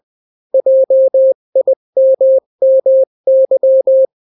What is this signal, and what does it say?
Morse code, heard as a keyed tone, about 20 wpm, 530 Hz. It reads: JIMMY
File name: daily / 18wpm